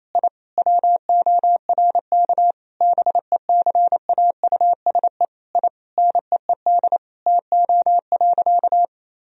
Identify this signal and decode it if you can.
Morse code, heard as a keyed tone, about 28 wpm, 700 Hz. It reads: I WORK BECAUSE I NEED TO.
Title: Sentences from Top 1000 Words - Review Part 3 - 28wpm